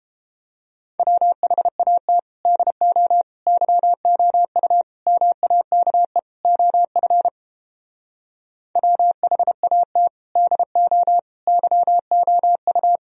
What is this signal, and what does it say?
Morse code, heard as a keyed tone, about 29 wpm, 700 Hz. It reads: WHAT DO YOU MAKE OF WHAT DO YOU